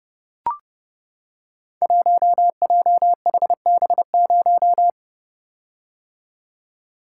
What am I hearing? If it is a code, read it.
Morse code, heard as a keyed tone, about 31 wpm, 700 Hz. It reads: 1JHB0
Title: Code Groups - 5 Letter:Numbers - 30wpm